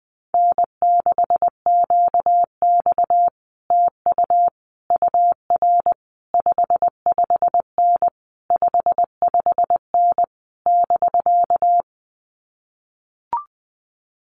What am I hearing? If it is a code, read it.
Morse code, heard as a keyed tone, about 20 wpm, 700 Hz. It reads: N6QX TU UR 55N 55N <BK>
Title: Common QSO Phrases Encore 20wpm